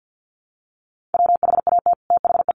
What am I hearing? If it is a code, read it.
Morse code, heard as a keyed tone, about 50 wpm, 700 Hz. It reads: WHAT THE
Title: Sets of 5 Words Encore 50wpm